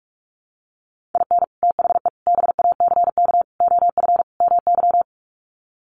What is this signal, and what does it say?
Morse code, heard as a keyed tone, about 44 wpm, 700 Hz. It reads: IN THE BACK OF MY
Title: Sets of 5 Words Encore 45wpm